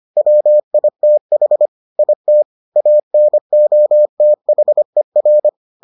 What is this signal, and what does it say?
Morse code, heard as a keyed tone, about 25 wpm, 600 Hz. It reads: WITH IT ANOTHER